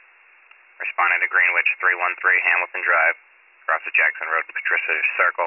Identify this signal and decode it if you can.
Emergency Alert System (SAME) tones, heard as harsh buzzing bursts, with speech. It is heard from 1.1 to 4.8 s.